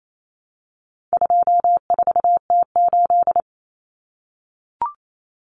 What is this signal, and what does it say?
Morse code, heard as a keyed tone, about 28 wpm, 700 Hz. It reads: 24T8
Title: Common Cut Numbers - 4-Digit 28wpm